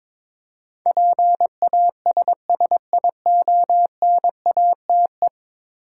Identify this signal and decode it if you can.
Morse code, heard as a keyed tone, about 22 wpm, 700 Hz. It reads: PASSIONATE
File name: practice / 22wpm / 10 Letter Words - First 3 Letters Given - 22wpm